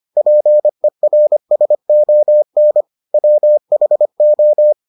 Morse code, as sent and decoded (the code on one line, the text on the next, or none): .--. .-. ... --- -. / .-- .... ---
PERSON WHO